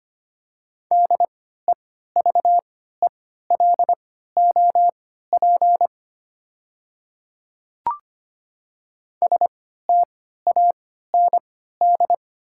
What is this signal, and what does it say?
Morse code, heard as a keyed tone, about 25 wpm, spaced as for 14 wpm, 700 Hz. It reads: DEVELOP STAND
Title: 3x Character Spacing - Top 500 Words - 25wpm